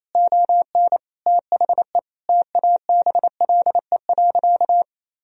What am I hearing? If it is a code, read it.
Morse code, heard as a keyed tone, about 28 wpm, 700 Hz. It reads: ON THE TABLE.
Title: Sentences from Top 1000 Words - Review Part 3 - 28wpm